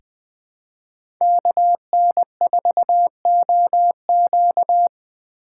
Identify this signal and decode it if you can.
Morse code, heard as a keyed tone, about 20 wpm, 700 Hz. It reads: KN4OQ
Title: USA Call Signs - 4x Character Spacing - 20wpm